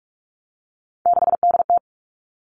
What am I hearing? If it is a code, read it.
Morse code, heard as a keyed tone, about 46 wpm, 700 Hz. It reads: 6DT
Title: Three Challenging Letter-Number Combinations 45wpm